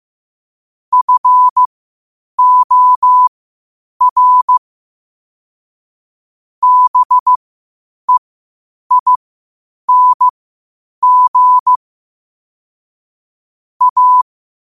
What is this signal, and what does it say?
Morse code, heard as a keyed tone, about 15 wpm, spaced as for 8 wpm, 1000 Hz. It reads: FOR BEING A
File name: qotdfiles / 8wpm